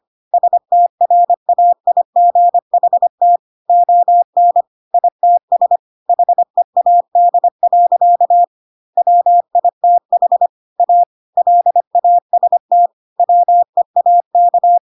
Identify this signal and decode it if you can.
Morse code, heard as a keyed tone, about 25 wpm, 700 Hz. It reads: STRAIGHT ON ITS HEAD. WITH A LAST WEAK
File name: Pinocchio / Pinocchio ch4 25 25 ik1hge